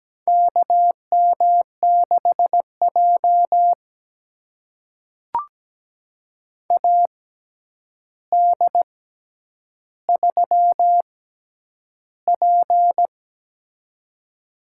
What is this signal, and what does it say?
Morse code, heard as a keyed tone, about 17 wpm, spaced as for 9 wpm, 700 Hz. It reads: KM6J AD3P
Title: USA Call Signs - 4 Characters - 6x Character Spacing - 17wpm